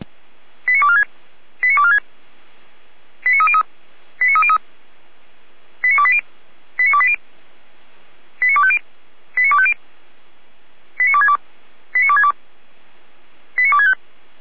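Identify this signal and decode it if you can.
Five-tone selective call (ZVEI), heard as quick runs of five tones, tones of about 80 ms. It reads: 89246 89246 89303 89303 89280 89280 89250 89250 89262 89262 89265
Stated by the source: ZVEI five-tone selective call, address 89235, received on SDR by Feuerwehr Orlamünde.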